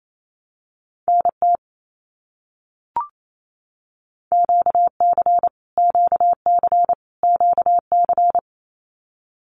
Morse code, heard as a keyed tone, about 28 wpm, 700 Hz. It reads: NT QC QC QC
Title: Canadian Provinces and Territories - Repeated 3x - 28wpm